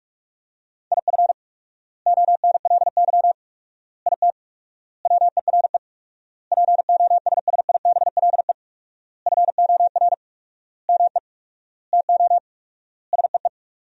Morse code, heard as a keyed tone, about 46 wpm, 700 Hz. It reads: IF ONLY IT WERE POSSIBLE FOR ME TO SEE
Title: 4x Word-Spacing - Sentences from Top 500 Words - 45wpm